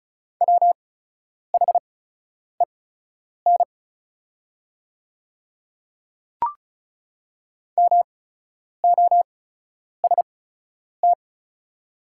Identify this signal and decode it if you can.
Morse code, heard as a keyed tone, about 36 wpm, spaced as for 10 wpm, 700 Hz. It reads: WHEN MOST
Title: 8x Character Spacing - Top 100 Words - 35wpm